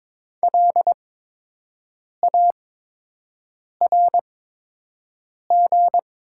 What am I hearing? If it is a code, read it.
Morse code, heard as a keyed tone, about 22 wpm, spaced as for 6 wpm, 700 Hz. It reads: LARG
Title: 8x Character Spacing - Top 200 Words - 22wpm